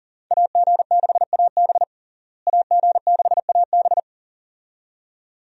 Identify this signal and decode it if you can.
Morse code, heard as a keyed tone, about 40 wpm, 700 Hz. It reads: AG6AB AG6AB
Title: USA Call Signs - 5 Characters - Repeated 2x - 40wpm